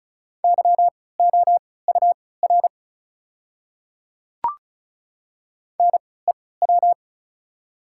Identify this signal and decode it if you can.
Morse code, heard as a keyed tone, about 34 wpm, spaced as for 20 wpm, 700 Hz. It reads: YOUR NEW
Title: 3x Character Spacing - Top 300 Words - 35wpm